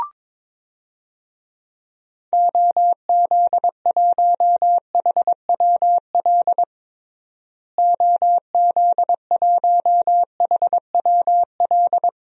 Morse code, heard as a keyed tone, about 22 wpm, 700 Hz. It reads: OZ1HWL OZ1HWL
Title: DXCC-100 Call Signs - 6 Characters - Repeated 3x - 22wpm